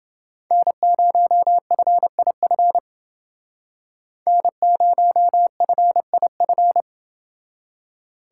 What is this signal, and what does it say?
Morse code, heard as a keyed tone, about 29 wpm, 700 Hz. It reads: N0FIF N0FIF